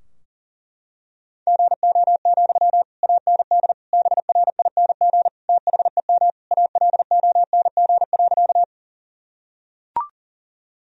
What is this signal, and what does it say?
Morse code, heard as a keyed tone, about 40 wpm, 700 Hz. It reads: GO, AND BRING THEM ALONG.
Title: Sentences from Top 1000 Words - Review Part 3 - 40wpm